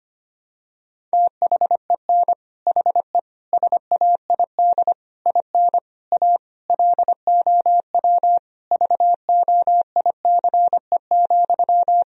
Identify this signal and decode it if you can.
Morse code, heard as a keyed tone, about 25 wpm, 700 Hz. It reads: THEN HE SAID IN A LOW VOICE,
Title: Sentences from Top 1000 Words - Review Part 1 - 25wpm